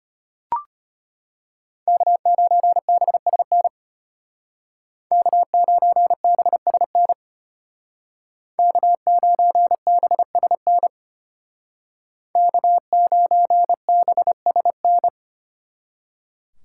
Morse code, heard as a keyed tone, about 32 wpm, 700 Hz. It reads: K9BSN K9BSN K9BSN K9BSN